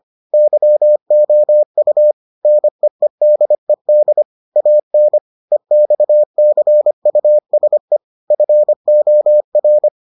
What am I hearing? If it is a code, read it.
Morse code, heard as a keyed tone, about 25 wpm, 600 Hz. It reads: YOU NEEDED AN EXCUSE FOR